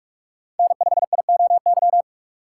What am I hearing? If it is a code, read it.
Morse code, heard as a keyed tone, about 46 wpm, 700 Hz. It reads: N5IOY
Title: USA Call Signs - 5 Characters - 3x Character Spacing - 45wpm